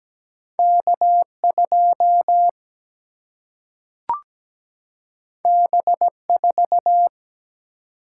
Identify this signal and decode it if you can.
Morse code, heard as a keyed tone, about 17 wpm, 700 Hz. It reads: K2 B4
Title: Single-Letter Single-Number 17wpm